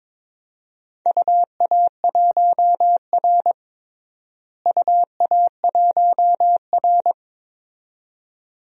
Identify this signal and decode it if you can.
Morse code, heard as a keyed tone, about 22 wpm, 700 Hz. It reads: UA1R UA1R
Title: DXCC-100 Call Signs - 4 Characters - Repeated 2x - 22wpm